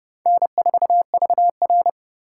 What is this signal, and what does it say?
Morse code, heard as a keyed tone, about 30 wpm, 700 Hz. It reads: N4VR
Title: USA Call Signs - 4 Characters - 3x Character Spacing - 30wpm